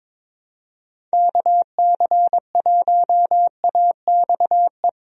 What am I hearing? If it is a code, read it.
Morse code, heard as a keyed tone, about 22 wpm, 700 Hz. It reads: KC1AXE
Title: USA Call Signs - 6 Characters - 22wpm